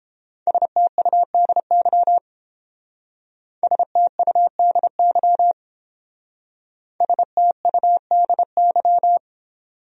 Morse code, heard as a keyed tone, about 29 wpm, 700 Hz. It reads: STUDY STUDY STUDY